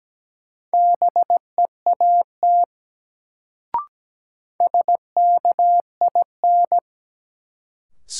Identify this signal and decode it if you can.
Morse code, heard as a keyed tone, about 17 wpm, 700 Hz. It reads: BEAT SKIN
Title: Sentences from Top 1300 Words - New Word Drill - 17wpm